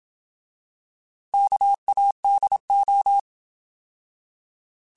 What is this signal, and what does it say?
Morse code, heard as a keyed tone, about 27 wpm, 800 Hz. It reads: KADO
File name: Alphabetical / Alphabetical mp3s / Ks @25